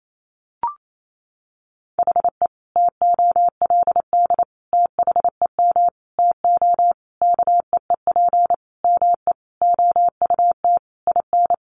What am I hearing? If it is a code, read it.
Morse code, heard as a keyed tone, about 28 wpm, 700 Hz. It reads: HE TOLD THEM TO KEEP ME OUT IN
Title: Sentences from Top 1200 Words 28wpm